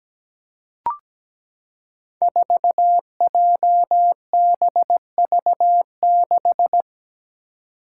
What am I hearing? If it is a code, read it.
Morse code, heard as a keyed tone, about 17 wpm, 700 Hz. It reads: 4JBV6
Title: Five Challenging Letter-Number Combinations 17wpm